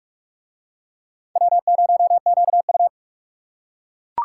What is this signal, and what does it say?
Morse code, heard as a keyed tone, about 41 wpm, 700 Hz. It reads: W0QU